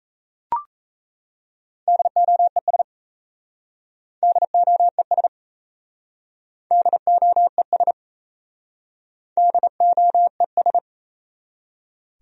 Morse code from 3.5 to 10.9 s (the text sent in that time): DOES DOES DOES